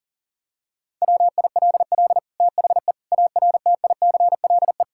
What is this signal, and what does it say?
Morse code, heard as a keyed tone, about 37 wpm, 700 Hz. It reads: WILL THE ARTICLE
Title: Sentences from Top 1000 Words - Review Part 5 - 40wpm